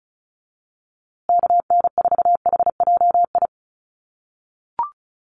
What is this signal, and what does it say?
Morse code, heard as a keyed tone, about 34 wpm, 700 Hz. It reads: KN4HJI
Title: USA Call Signs Repeated 2x 35wpm